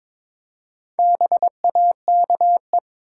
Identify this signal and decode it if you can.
Morse code, heard as a keyed tone, about 23 wpm, 700 Hz. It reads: BAKE
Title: Four Letter Words - 22wpm